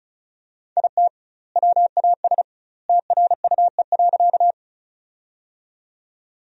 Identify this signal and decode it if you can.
Morse code, heard as a keyed tone, about 36 wpm, 700 Hz. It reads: IT WAS TRUE.
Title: 2x Word-Spacing - Sentences from Top 400 Words - 35wpm